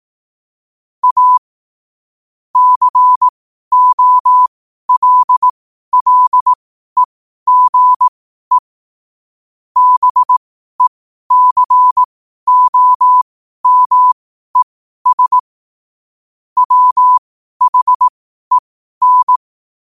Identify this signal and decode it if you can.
Morse code, heard as a keyed tone, about 18 wpm, spaced as for 12 wpm, 1000 Hz. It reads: A COLLEGE BECOMES WHEN
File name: qotdfiles / 12wpm